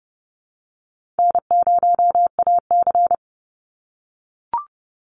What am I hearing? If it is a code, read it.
Morse code, heard as a keyed tone, about 30 wpm, 700 Hz. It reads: N0AC
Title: Easy Call Signs - 30wpm